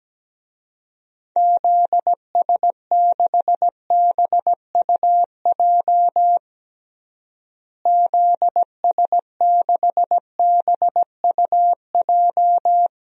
Morse code, heard as a keyed tone, about 17 wpm, 700 Hz. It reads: ZS6BUJ ZS6BUJ
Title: DXCC-100 Call Signs - 6 Characters - Repeated 3x - 17wpm